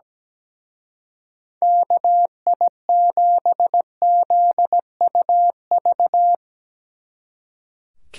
Morse code, heard as a keyed tone, about 18 wpm, 700 Hz. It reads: KI7ZUV
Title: Speed Racing - USA 2x3 Call Signs 17wpm